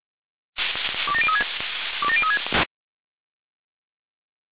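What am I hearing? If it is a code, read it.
Five-tone selective call (ZVEI), heard as quick runs of five tones, tones of about 70 ms. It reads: 27036 27036
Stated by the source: Five-tone ZVEI selective call received by SDR.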